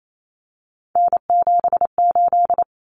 Morse code, heard as a keyed tone, about 28 wpm, 700 Hz. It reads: N78
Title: Common Cut Numbers - 3-Digit 28wpm